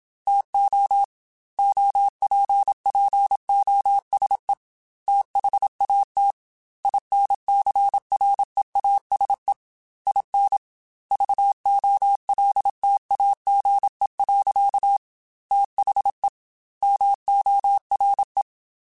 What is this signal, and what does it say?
Morse code, heard as a keyed tone, about 27 wpm, 800 Hz. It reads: TO OPPOSE THAT INCREASE IN VOLTAGE. THE MORE
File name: Technical Subjects / Technical Subjects-mp3s / XC @25